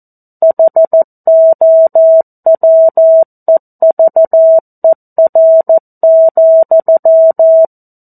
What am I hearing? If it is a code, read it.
Morse code, heard as a keyed tone, about 14 wpm, 650 Hz. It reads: HOWEVER,
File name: 02-Chapter-02 14 WPM